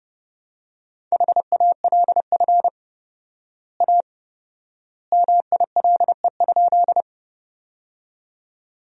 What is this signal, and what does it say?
Morse code, heard as a keyed tone, about 29 wpm, 700 Hz. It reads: HALF A MILE?
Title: 4x Word-Spacing - Sentences from Top 400 Words - 30wpm